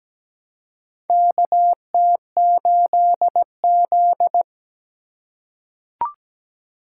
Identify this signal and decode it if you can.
Morse code, heard as a keyed tone, about 17 wpm, 700 Hz. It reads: KT8Z
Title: USA Call Signs - 4 Characters - 3x Character Spacing - 17wpm